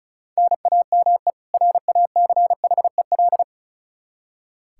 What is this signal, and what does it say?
Morse code, heard as a keyed tone, about 36 wpm, 700 Hz. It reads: NAME RACHEL
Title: Practice QSO - Exchange 1 - Beginning - 35wpm